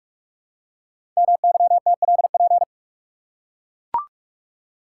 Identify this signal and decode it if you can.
Morse code, heard as a keyed tone, about 44 wpm, 700 Hz. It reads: MYTLP
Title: Code Groups - 5 Letters - 45wpm